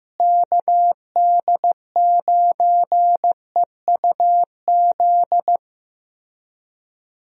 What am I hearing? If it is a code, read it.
Morse code, heard as a keyed tone, about 15 wpm, 700 Hz. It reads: KD9EUZ
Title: USA Call Signs - 6 Characters - Repeated 3x - 15wpm